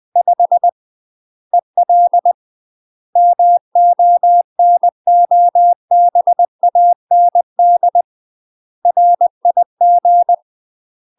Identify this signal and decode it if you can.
Morse code, heard as a keyed tone, about 20 wpm, 700 Hz. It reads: 5 EL MONOBAND RIG
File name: UK MASTER QSOs TRAINING / 20WPM / UK MASTER QSOs 0003